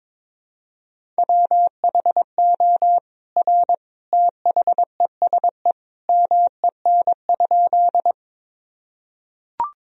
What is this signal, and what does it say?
Morse code, heard as a keyed tone, about 22 wpm, 700 Hz. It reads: WHO R THESE MEN?
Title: Sentences from Top 200 Words and Abbreviations 22wpm